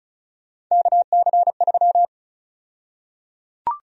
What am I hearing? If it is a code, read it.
Morse code, heard as a keyed tone, about 36 wpm, 700 Hz. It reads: KC3
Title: USA Call Sign Prefixes - 35wpm